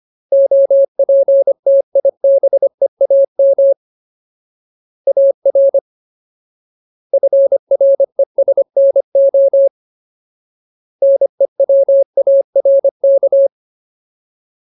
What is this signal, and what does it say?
Morse code, heard as a keyed tone, about 25 wpm, 550 Hz. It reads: OPTIBEAM AR FRESNO NEWARK
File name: Ultimate Word List / 25WPM Ultimate